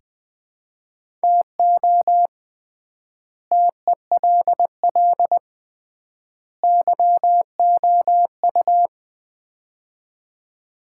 Morse code, heard as a keyed tone, about 20 wpm, 700 Hz. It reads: TO TELL YOU
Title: 3x Word-Spacing - Sets of 3 Words - 20wpm